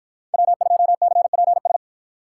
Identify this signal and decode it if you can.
Morse code, heard as a keyed tone, about 53 wpm, 700 Hz. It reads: W2CPS